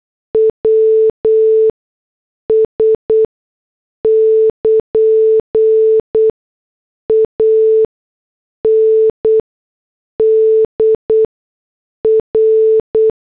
Morse code, heard as a keyed tone, about 8 wpm, spaced as for 6 wpm, 440 Hz. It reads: WS <KN> AND R